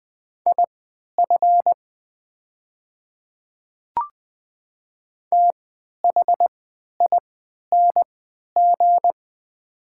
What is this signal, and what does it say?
Morse code, heard as a keyed tone, about 20 wpm, spaced as for 11 wpm, 700 Hz. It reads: IF THING
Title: 3x Character Spacing - Top 200 Words - 20wpm